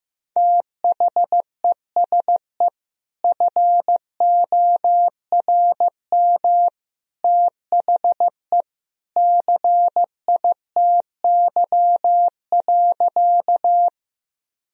Morse code, heard as a keyed tone, about 15 wpm, 700 Hz. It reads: THESE FORM THE CITY.
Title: Sentences from Top 1000 Words - Review Part 1 - 15wpm